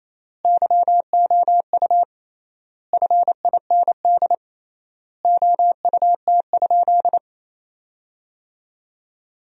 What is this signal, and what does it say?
Morse code, heard as a keyed tone, about 28 wpm, 700 Hz. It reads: YOU FIND OUT?
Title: 3x Word-Spacing - Sentences from Top 100 Words - 28wpm